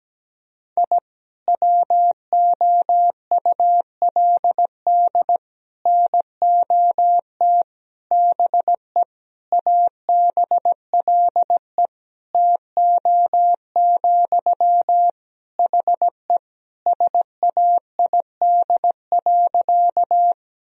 Morse code, heard as a keyed tone, about 17 wpm, 700 Hz. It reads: I WOULD NOT BE ABLE TO, HE SAID.
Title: Sentences from Top 1000 Words - Review Part 2 - 17wpm